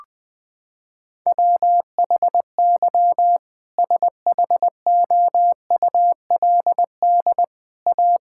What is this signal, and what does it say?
Morse code, heard as a keyed tone, about 21 wpm, 700 Hz. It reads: WHY SHOULD A
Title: Sentences from Top 1600 Words 20wpm